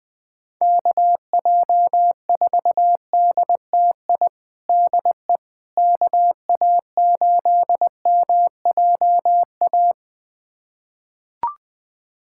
Morse code, heard as a keyed tone, about 21 wpm, 700 Hz. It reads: KJ4DTI DE KA8MJA
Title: USA Callsign DE Callsign 20wpm